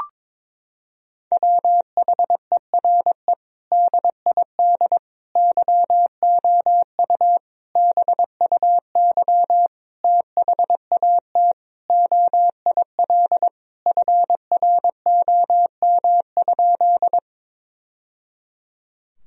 Morse code, heard as a keyed tone, about 22 wpm, 700 Hz. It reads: WHERE DID YOU BUY THAT OIL FROM?
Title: Sentences from Top 1000 Words - Review Part 4 - 22wpm